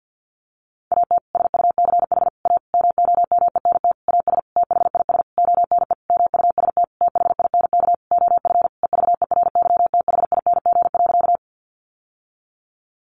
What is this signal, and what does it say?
Morse code, heard as a keyed tone, about 50 wpm, 700 Hz. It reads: AT SUCH A MOMENT AS THIS ONE MUST THINK OF EVERYTHING.
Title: Sentences from Top 1000 Words 50wpm